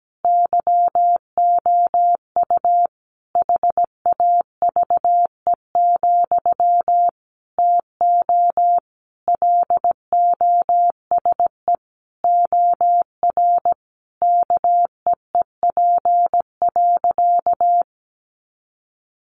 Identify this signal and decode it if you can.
Morse code, heard as a keyed tone, about 17 wpm, 700 Hz. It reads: YOU HAVE, TO LOSE OR KEEP.
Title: Sentences from Top 1100 Words 17wpm